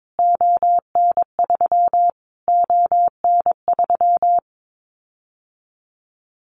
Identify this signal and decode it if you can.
Morse code, heard as a keyed tone, about 22 wpm, 700 Hz. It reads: ON3 ON3
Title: Callsign Prefixes for Top 25 DXCC - Repeated 3x - 22wpm